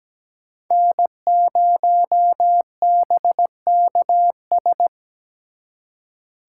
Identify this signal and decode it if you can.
Morse code, heard as a keyed tone, about 20 wpm, 700 Hz. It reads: N0BKS
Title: Speed Racing - USA 1x3 Call Signs 17wpm